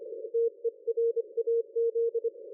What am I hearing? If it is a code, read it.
Morse code, heard as a keyed tone, about 25 wpm, 460 Hz. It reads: TERAZ
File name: dlazaawansowanych / nagraniazpasma / ExpertSDR2 line out 23.01.2020 14.12.00